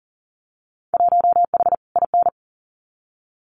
Morse code, heard as a keyed tone, about 37 wpm, 700 Hz. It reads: 1H IN
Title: Winter Field Day Exchanges 40wpm